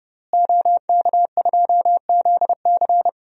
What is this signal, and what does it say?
Morse code, heard as a keyed tone, about 30 wpm, 700 Hz. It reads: OK2ZC
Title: DXCC-100 Call Signs - 5 Characters - Repeated 2x - 30wpm